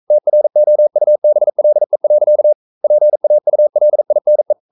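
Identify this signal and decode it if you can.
Morse code, heard as a keyed tone, about 42 wpm, 600 Hz. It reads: TROUBLE. PAULINE
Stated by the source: Morse code sending of an English novel at 42 wpm.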